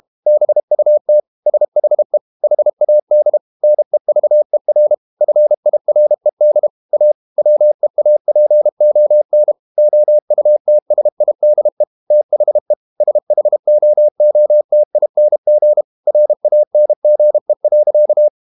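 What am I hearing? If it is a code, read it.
Morse code, heard as a keyed tone, about 32 wpm, 600 Hz. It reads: BUT SHE HAD NEVER FIRED A WEAPON OUTSIDE THE SHOOTING RANGE.
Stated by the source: Morse code practice at 32 wpm from CW med Gnister, reading Ken Follett's novel Never in English.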